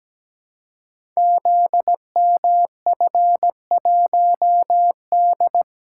Morse code, heard as a keyed tone, about 17 wpm, 700 Hz. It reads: ZMF1D
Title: Code Groups - 5 Letter:Numbers - 17wpm